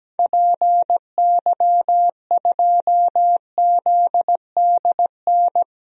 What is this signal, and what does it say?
Morse code, heard as a keyed tone, about 17 wpm, 700 Hz. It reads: PY2ZDN
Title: DXCC-100 Call Signs - 6 Characters - Repeated 3x - 17wpm